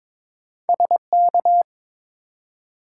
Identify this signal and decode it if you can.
Morse code, heard as a keyed tone, about 22 wpm, 700 Hz. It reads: SK
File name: ARRL Sections v2 - 22wpm